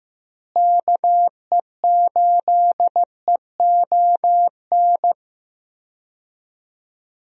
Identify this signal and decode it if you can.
Morse code, heard as a keyed tone, about 15 wpm, 700 Hz. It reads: KE8EON
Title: USA Call Signs - 6 Characters - Repeated 2x - 15wpm